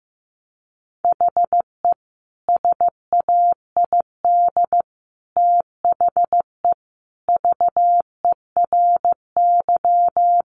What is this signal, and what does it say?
Morse code, heard as a keyed tone, about 15 wpm, 700 Hz. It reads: HE SAID THE VERY
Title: Sentences from Top 1300 Words 15wpm